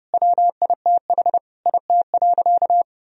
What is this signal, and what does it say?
Morse code, heard as a keyed tone, about 30 wpm, 700 Hz. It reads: WITH IT.
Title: Sentences from Top 1000 Words - Review Part 1 - 30wpm